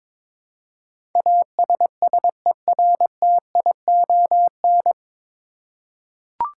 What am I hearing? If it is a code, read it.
Morse code, heard as a keyed tone, about 22 wpm, 700 Hz. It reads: ASSERTION
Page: Morse Code Ninja practice 22wpm